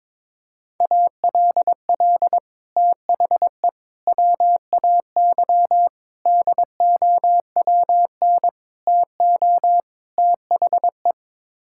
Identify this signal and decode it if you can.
Morse code, heard as a keyed tone, about 22 wpm, 700 Hz. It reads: ALL THE WAY DOWN TO THE
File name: practice / 22wpm / Sets of 6 Words - 22wpm